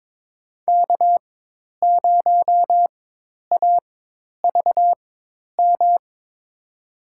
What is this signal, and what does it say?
Morse code, heard as a keyed tone, about 22 wpm, spaced as for 10 wpm, 700 Hz. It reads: K0AVM
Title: USA Call Signs - 5 Characters - 4x Character Spacing - 22wpm